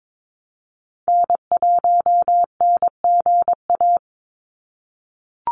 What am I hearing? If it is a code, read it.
Morse code, heard as a keyed tone, about 22 wpm, 700 Hz. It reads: N1NGA
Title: USA Call Signs 22wpm